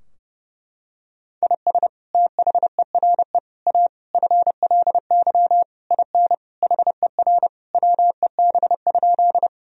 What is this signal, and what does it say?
Morse code, heard as a keyed tone, about 30 wpm, 700 Hz. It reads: IS THERE A FLY IN HER WEB?
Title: Sentences from Top 1000 Words - Review Part 4 - 30wpm